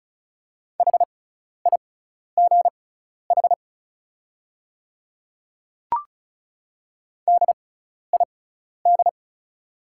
Morse code, heard as a keyed tone, about 36 wpm, spaced as for 12 wpm, 700 Hz. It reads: HIGH DID
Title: 6x Character Spacing - Top 200 Words - 35wpm